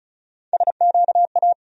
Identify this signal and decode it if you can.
Morse code, heard as a keyed tone, about 34 wpm, 700 Hz. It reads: SQA
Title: USA Call Sign Postfixes - 3 Letters 35wpm